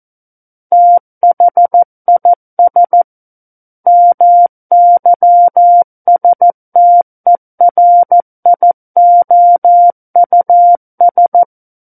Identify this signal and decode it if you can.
Morse code, heard as a keyed tone, about 14 wpm, 700 Hz. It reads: THIS MYSTERIOUS
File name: Book One / 05-Chapter 5 - 14 WPM